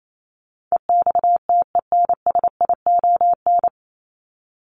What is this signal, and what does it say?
Morse code, heard as a keyed tone, about 28 wpm, 700 Hz. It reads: EXTENSION
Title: Top 1501-1750 Words 28wpm